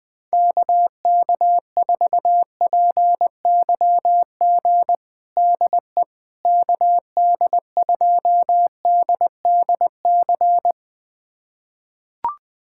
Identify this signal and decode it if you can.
Morse code, heard as a keyed tone, about 20 wpm, 700 Hz. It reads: KK4PYG DE KD2DDC
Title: USA Callsign DE Callsign 20wpm